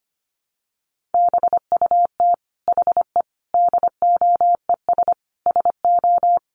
Morse code, heard as a keyed tone, about 25 wpm, 700 Hz. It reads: BUT HE DOES SO
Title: Sentences from Top 200 Words 25wpm